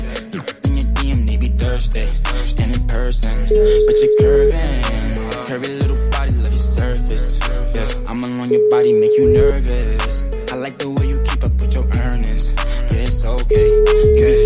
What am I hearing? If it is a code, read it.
Ringback tone (Europe, 425 Hz), with noise.